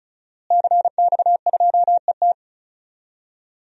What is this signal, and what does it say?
Morse code, heard as a keyed tone, about 36 wpm, 700 Hz. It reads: CX2ET
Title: DXCC-100 Call Signs - 5 Characters - Repeated 3x - 35wpm